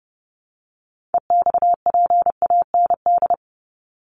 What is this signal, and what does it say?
Morse code, heard as a keyed tone, about 30 wpm, 700 Hz. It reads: EXPAND
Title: Top 1251-1500 Words 30wpm